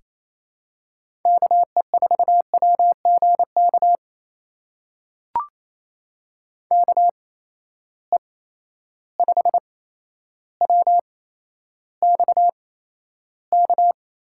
Morse code, heard as a keyed tone, about 28 wpm, spaced as for 12 wpm, 700 Hz. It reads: KE4WGK KE5WXK
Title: USA Call Signs - 6 Characters - 8x Character Spacing - 28wpm